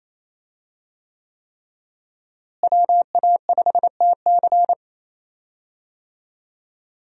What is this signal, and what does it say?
Morse code, heard as a keyed tone, about 28 wpm, 700 Hz. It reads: WA5TC
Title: USA Call Signs - 5 Characters - Repeated 3x - 28wpm